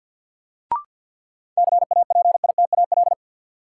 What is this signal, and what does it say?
Morse code, heard as a keyed tone, about 52 wpm, 700 Hz. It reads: CAPITAL
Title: Sentences from Top 1000 Words - New Word Drill - 50wpm